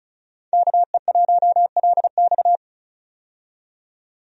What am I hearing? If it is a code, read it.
Morse code, heard as a keyed tone, about 36 wpm, 700 Hz. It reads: KE1LX
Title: USA Call Signs - 5 Characters - 35wpm